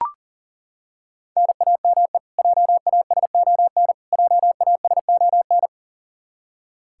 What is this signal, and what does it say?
Morse code, heard as a keyed tone, about 40 wpm, 700 Hz. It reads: NAME JASON JASON
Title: Practice QSO - Exchange 1 - Beginning - 40wpm